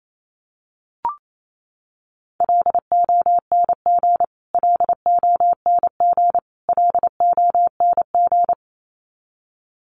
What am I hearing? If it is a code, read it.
Morse code, heard as a keyed tone, about 27 wpm, 700 Hz. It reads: LONG LONG LONG
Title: Top 500 Words Repeated 28wpm